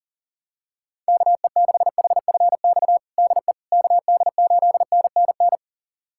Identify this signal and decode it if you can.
Morse code, heard as a keyed tone, about 37 wpm, 700 Hz. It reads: KE6HFX DE KD8NNN
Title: USA Callsign DE Callsign - 6 Characters - 40wpm